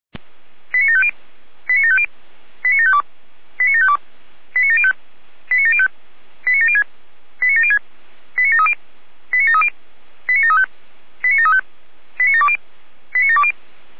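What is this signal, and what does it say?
Five-tone selective call (ZVEI), heard as quick runs of five tones, tones of about 70 ms. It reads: 89750 89750 89752 89752 89705 89705 89706 89706 89730 89730 89735 89735 89720 89720
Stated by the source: ZVEI five-tone call received by SDR.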